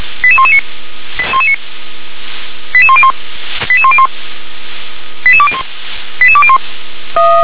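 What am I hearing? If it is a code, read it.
Five-tone selective call (ZVEI), heard as quick runs of five tones, tones of about 70 ms. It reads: 88119 88101 88101 88201 88201
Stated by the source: ZVEI five-tone selective call 88126 received on SDR.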